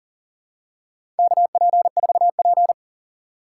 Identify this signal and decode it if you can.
Morse code, heard as a keyed tone, about 40 wpm, 700 Hz. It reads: KP4P
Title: USA Call Signs - 4 Characters - 2x Character Spacing - 40wpm